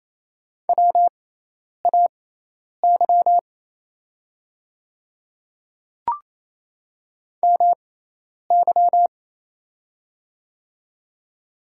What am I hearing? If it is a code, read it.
Morse code, heard as a keyed tone, about 28 wpm, spaced as for 10 wpm, 700 Hz. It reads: WAY MY
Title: 6x Character Spacing - Top 100 Words - 28wpm